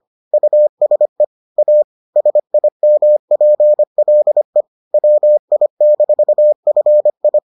Morse code, heard as a keyed tone, about 25 wpm, 600 Hz. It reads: USE A SIMPLE WI-FI